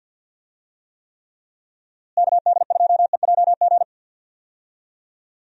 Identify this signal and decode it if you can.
Morse code, heard as a keyed tone, about 48 wpm, 700 Hz. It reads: KD2EJG
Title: USA Call Signs - 6 Characters - Repeated 3x - 50wpm